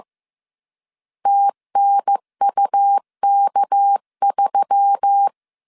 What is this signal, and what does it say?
Morse code, heard as a keyed tone, about 15 wpm, 800 Hz. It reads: TNUK3
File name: CW / DET / 1 6 1500C S C 14wpm 2951